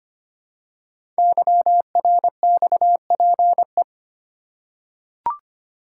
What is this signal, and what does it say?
Morse code, heard as a keyed tone, about 25 wpm, 700 Hz. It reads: YRXPE
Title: Code Groups - 5 Letters - 25wpm